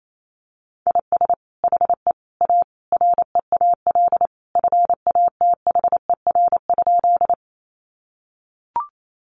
Morse code, heard as a keyed tone, about 28 wpm, 700 Hz. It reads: IS HE A REAL FATHER?
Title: Sentences from Top 300 Words 28wpm